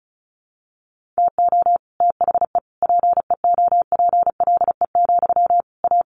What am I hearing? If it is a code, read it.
Morse code, heard as a keyed tone, about 34 wpm, 700 Hz. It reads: TO THE PEOPLE, A
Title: Sentences from Top 1500 Words 35wpm